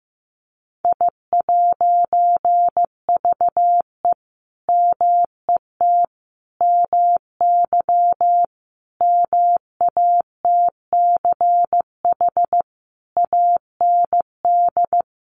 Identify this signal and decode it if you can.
Morse code, heard as a keyed tone, about 15 wpm, 700 Hz. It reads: I'VE MET MY MATCH AND
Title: A Sound Of Thunder 15wpm